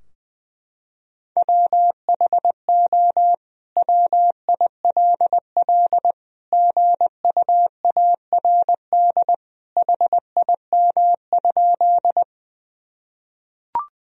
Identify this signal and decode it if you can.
Morse code, heard as a keyed tone, about 20 wpm, 700 Hz. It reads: WHO WILL GUARD HIM?
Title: Sentences from Top 1000 Words - Review Part 4 - 20wpm